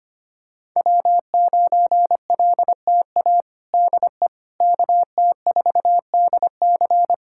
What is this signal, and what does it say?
Morse code, heard as a keyed tone, about 25 wpm, 700 Hz. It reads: W9LTA DE KT4DC
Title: USA Callsign DE Callsign - 5 Characters - 25wpm